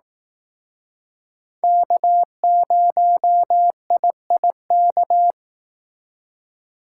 Morse code, heard as a keyed tone, about 18 wpm, 700 Hz. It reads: K0IIK
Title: Speed Racing - USA 1x3 Call Signs 15wpm